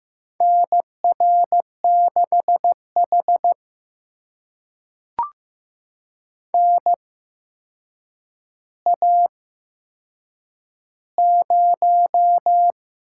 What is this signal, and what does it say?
Morse code, heard as a keyed tone, about 15 wpm, spaced as for 7 wpm, 700 Hz. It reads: NR6H NA0
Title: USA Call Signs - 4 Characters - 8x Character Spacing - 15wpm